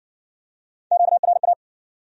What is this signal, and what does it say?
Morse code, heard as a keyed tone, about 56 wpm, 700 Hz. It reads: QRA